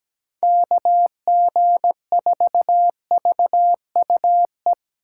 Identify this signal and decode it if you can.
Morse code, heard as a keyed tone, about 20 wpm, 700 Hz. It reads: KG4VUE
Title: Speed Racing - USA 2x3 Call Signs 17wpm